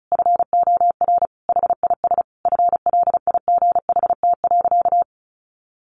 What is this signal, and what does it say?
Morse code, heard as a keyed tone, about 34 wpm, 700 Hz. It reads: FOR HIS FLIGHT.
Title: Sentences from Top 1400 Words 35wpm